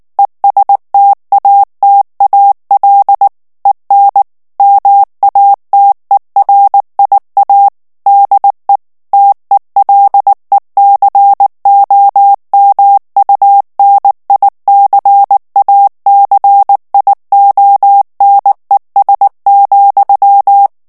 Morse code, heard as a keyed tone, about 19 wpm, 800 Hz. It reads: ESTATAL EN MATERIA DE TELECOMUNICACIONES,
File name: morse mp3 / ESPAGNOL TEXTE EN CLAIR 2 VITESSE 18 MOTS MINUTES